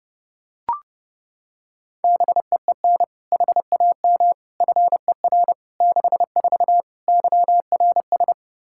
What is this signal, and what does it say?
Morse code, heard as a keyed tone, about 30 wpm, 700 Hz. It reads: BEEN HAM FER 64 YRS
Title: K1USN SST QSO Practice 30wpm